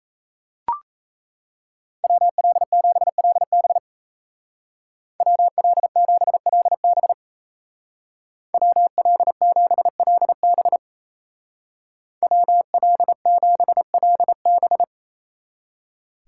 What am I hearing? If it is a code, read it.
Morse code, heard as a keyed tone, about 34 wpm, 700 Hz. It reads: WL7LB WL7LB WL7LB WL7LB
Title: Speed Racing - USA 2x2 Call Signs 28wpm